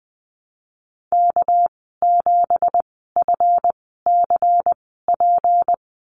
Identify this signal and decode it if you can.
Morse code, heard as a keyed tone, about 20 wpm, spaced as for 15 wpm, 700 Hz. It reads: K7FCP
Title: 200 Lesson 32g - Letter k - Call Signs 20wpm